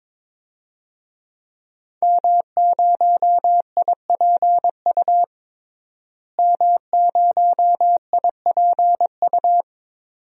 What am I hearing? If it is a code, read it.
Morse code, heard as a keyed tone, about 21 wpm, 700 Hz. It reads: M0IPU M0IPU
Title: DXCC-100 Call Signs - 5 Characters - Repeated 2x - 22wpm